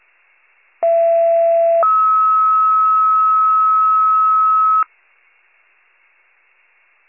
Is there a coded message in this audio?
Yes: two-tone page from 0.8 to 4.8 s, heard as a one-second tone, then a three-second tone, about 670 then 1290 Hz.